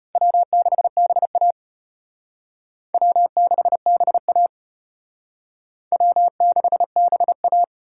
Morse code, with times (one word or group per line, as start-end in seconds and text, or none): 0.2-1.5 W6BA
2.9-4.5 W6BA
5.9-7.6 W6BA